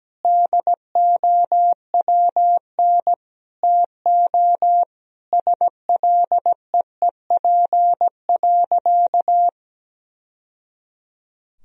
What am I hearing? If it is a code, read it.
Morse code, heard as a keyed tone, about 17 wpm, 700 Hz. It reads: DOWN TO SLEEP.
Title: Sentences from Top 1000 Words - Review Part 2 - 17wpm